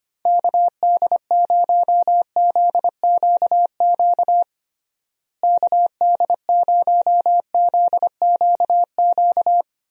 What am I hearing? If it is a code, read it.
Morse code, heard as a keyed tone, about 25 wpm, 700 Hz. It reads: KD0ZQQ KD0ZQQ
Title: USA Call Signs - 6 Characters - Repeated 2x - 25wpm